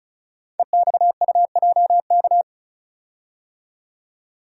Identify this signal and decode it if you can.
Morse code, heard as a keyed tone, about 36 wpm, 700 Hz. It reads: EXUJK
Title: Code Groups - 5 Letters - 35wpm